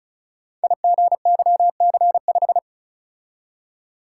Morse code, heard as a keyed tone, about 36 wpm, 700 Hz. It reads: IGYC5
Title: Code Groups - 5 Letter:Numbers - 35wpm